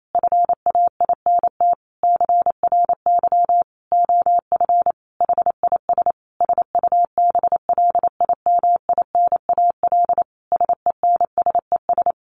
Morse code, heard as a keyed tone, about 28 wpm, 700 Hz. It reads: FAINT CRY OF HIS SUBLIMINAL SENSES
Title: A Sound Of Thunder 28wpm